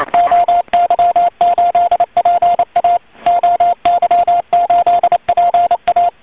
Morse code, heard as a keyed tone, about 28 wpm, 690 Hz, with radio noise. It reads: OY8PA OY8PA
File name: oy 2003 / real-audio / 160m-oy8pa-cw-10.5min